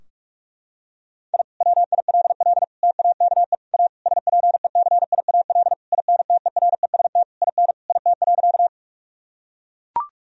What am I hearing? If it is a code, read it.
Morse code, heard as a keyed tone, about 42 wpm, 700 Hz. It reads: I WILL TAKE A SPECIAL INTEREST IN IT.
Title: Sentences from Top 1000 Words - Review Part 3 - 45wpm